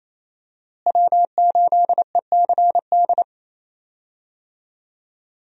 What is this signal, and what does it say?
Morse code, heard as a keyed tone, about 27 wpm, 700 Hz. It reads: W8ECD
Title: USA Call Signs - 5 Characters - Repeated 3x - 28wpm